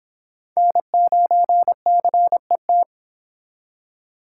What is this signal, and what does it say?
Morse code, heard as a keyed tone, about 26 wpm, 700 Hz. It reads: N9CET